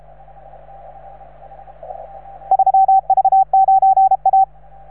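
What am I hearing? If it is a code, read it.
Morse code, heard as a keyed tone, about 35 wpm, 770 Hz. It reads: 3V9A
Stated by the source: Morse from station 3V9A.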